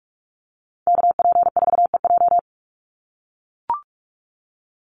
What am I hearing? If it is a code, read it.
Morse code, heard as a keyed tone, about 44 wpm, 700 Hz. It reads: KP4EJ
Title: Call Signs - 45wpm